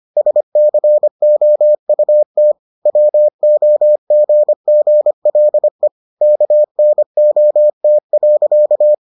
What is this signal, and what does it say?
Morse code, heard as a keyed tone, about 25 wpm, 600 Hz. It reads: SCOUT WOGGLE KNOT.